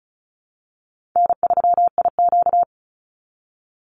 Morse code, heard as a keyed tone, about 34 wpm, 700 Hz. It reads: N3IQ
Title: Easy Call Signs Repeated 2x - 35wpm